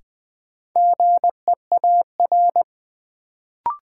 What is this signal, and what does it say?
Morse code, heard as a keyed tone, about 20 wpm, 700 Hz. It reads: GEAR
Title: Sentences from Top 1400 Words - New Word Drill - 20wpm